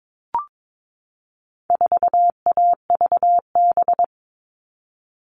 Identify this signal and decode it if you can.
Morse code, heard as a keyed tone, about 22 wpm, 700 Hz. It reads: 4AVB
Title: All Cut Numbers - 4-Digit 22wpm